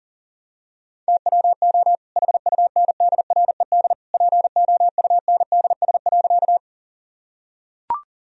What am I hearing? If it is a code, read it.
Morse code, heard as a keyed tone, about 40 wpm, 700 Hz. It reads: TWO HUNDRED POUNDS.
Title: Sentences from Top 1000 Words - Review Part 2 - 40wpm